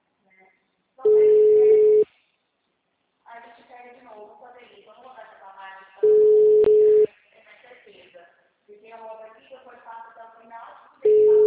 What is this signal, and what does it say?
Ringback tone (Europe, 425 Hz), with speech.